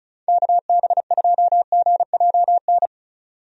Morse code, heard as a keyed tone, about 36 wpm, 700 Hz. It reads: KB2GJN
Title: USA Call Signs - 6 Characters - 35wpm